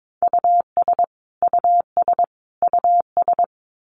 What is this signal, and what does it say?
Morse code, heard as a keyed tone, about 22 wpm, 700 Hz. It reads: US US US
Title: Top 500 Words Repeated 22wpm